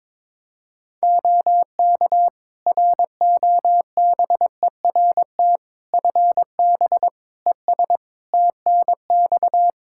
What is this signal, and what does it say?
Morse code, heard as a keyed tone, about 22 wpm, 700 Hz. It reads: OK ROBERT FB ES TNX
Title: K1USN SST QSO Practice 22wpm